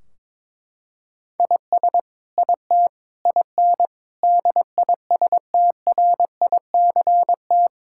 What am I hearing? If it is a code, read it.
Morse code, heard as a keyed tone, about 22 wpm, 700 Hz. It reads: IS IT IN DISTRICT
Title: Sentences from Top 1600 Words 22wpm